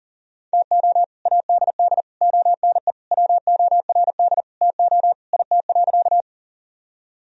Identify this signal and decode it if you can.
Morse code, heard as a keyed tone, about 40 wpm, 700 Hz. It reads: TO ADD ONE WORD TO IT.
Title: Sentences from Top 1000 Words - Review Part 1 - 40wpm